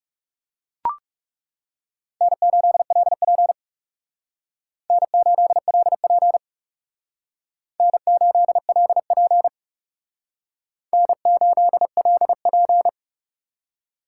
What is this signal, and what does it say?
Morse code, heard as a keyed tone, about 39 wpm, 700 Hz. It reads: N8LP N8LP N8LP N8LP